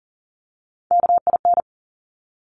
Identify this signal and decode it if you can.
Morse code, heard as a keyed tone, about 41 wpm, 690 Hz. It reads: KIN
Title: Top Three Letter Combinations in Words 40wpm